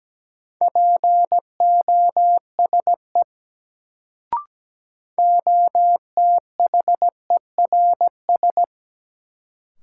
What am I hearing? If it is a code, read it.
Morse code, heard as a keyed tone, about 17 wpm, 700 Hz. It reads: POSE OTHERS
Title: Sentences from Top 1000 Words - Review Part 2 - New Word Drill - 17wpm